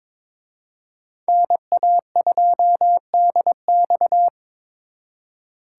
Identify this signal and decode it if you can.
Morse code, heard as a keyed tone, about 26 wpm, 700 Hz. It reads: NA2DX
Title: Speed Racing - USA Call Signs 22wpm